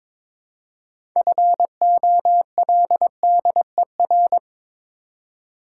Morse code, heard as a keyed tone, about 22 wpm, 700 Hz. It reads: FOLDER
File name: Difficult Words with Hints - 22wpm